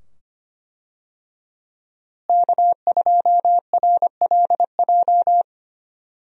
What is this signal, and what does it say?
Morse code, heard as a keyed tone, about 25 wpm, 700 Hz. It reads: K2RLJ
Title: USA Call Signs - 5 Characters - Repeated 2x - 25wpm